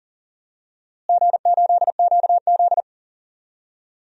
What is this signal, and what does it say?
Morse code, heard as a keyed tone, about 38 wpm, 700 Hz. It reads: G8QZ
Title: DXCC-100 Call Signs - 4 Characters - Repeated 2x - 40wpm